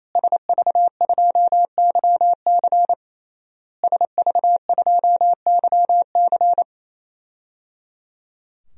Morse code, heard as a keyed tone, about 27 wpm, 700 Hz. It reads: SV2YC SV2YC
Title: DXCC-100 Call Signs - 5 Characters - Repeated 3x - 28wpm